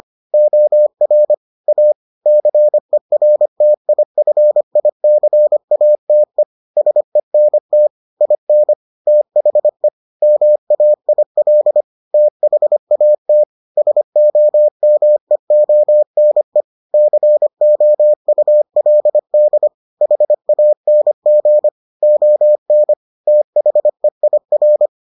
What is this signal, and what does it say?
Morse code, heard as a keyed tone, about 25 wpm, 600 Hz. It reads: OR A CERTIFICATE SENT IN THE MAIL THAT SOMEONE COULD HANG ON THEIR